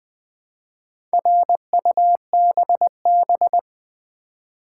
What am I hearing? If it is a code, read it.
Morse code, heard as a keyed tone, about 20 wpm, 700 Hz. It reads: RUBB
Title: Build A Word - 2 Letters At A Time - 6-Letter Words - 20wpm